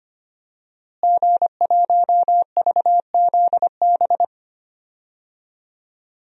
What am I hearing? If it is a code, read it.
Morse code, heard as a keyed tone, about 25 wpm, 700 Hz. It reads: G1VZB
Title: DXCC-100 Call Signs - 5 Characters - Repeated 3x - 25wpm